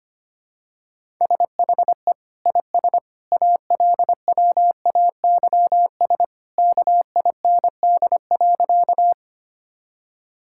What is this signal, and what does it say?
Morse code, heard as a keyed tone, about 25 wpm, 700 Hz. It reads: SHE IS ALWAYS KIND.
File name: practice / 25wpm / Sentences with 2-Word Phrases - 25wpm